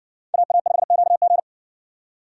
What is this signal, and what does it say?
Morse code, heard as a keyed tone, about 57 wpm, 700 Hz. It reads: AA5QG